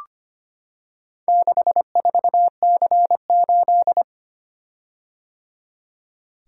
Morse code, heard as a keyed tone, about 25 wpm, 700 Hz. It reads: 64C8
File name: practice / 25wpm / Four Letter-Number Combinations - 25wpm